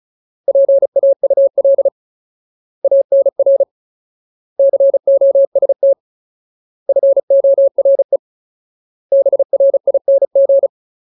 Morse code, heard as a keyed tone, about 34 wpm, 550 Hz. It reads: PAUL ANR COST FORE BRING
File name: Ultimate Word List / 35WPM Ultimate